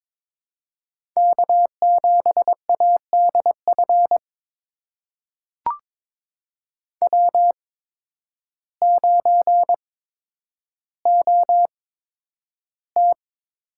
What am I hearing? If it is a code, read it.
Morse code, heard as a keyed tone, about 23 wpm, spaced as for 10 wpm, 700 Hz. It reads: K7ADF W9OT
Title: USA Call Signs - 5 Characters - 8x Character Spacing - 22wpm